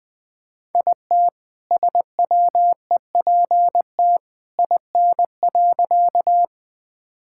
Morse code, heard as a keyed tone, about 21 wpm, 700 Hz. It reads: IT SWEPT IN.
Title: All-Age Appropriate Jokes - 20wpm